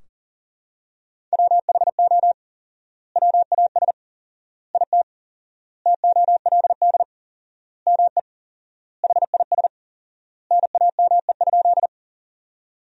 Morse code, heard as a keyed tone, about 40 wpm, 700 Hz. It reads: WHO WAS IT TOLD ME HIS NAME?
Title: 4x Word-Spacing - Sentences from Top 400 Words - 40wpm